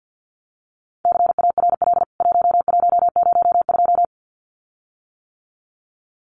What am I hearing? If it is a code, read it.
Morse code, heard as a keyed tone, about 50 wpm, 700 Hz. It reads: CARL 1102
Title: CWT Exchange 50wpm